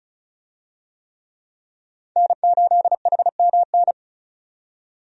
Morse code, heard as a keyed tone, about 36 wpm, 700 Hz. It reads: N8HMN
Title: USA Call Signs - 5 Characters - 35wpm